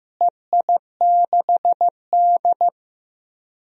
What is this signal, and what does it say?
Morse code, heard as a keyed tone, about 15 wpm, 700 Hz. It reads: EI6D
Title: DXCC-100 Call Signs - 4 Characters - Repeated 2x - 15wpm